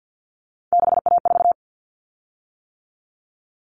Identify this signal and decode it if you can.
Morse code, heard as a keyed tone, about 50 wpm, 700 Hz. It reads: 6A4
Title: Common Cut Numbers - 3-Digit 50wpm